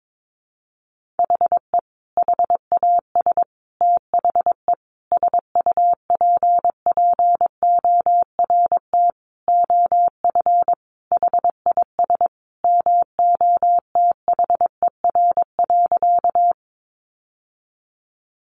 Morse code, heard as a keyed tone, about 22 wpm, 700 Hz. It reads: HE HAS THE SUPPORT OF HIS MOTHER.
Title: Sentences from Top 800 Words 22wpm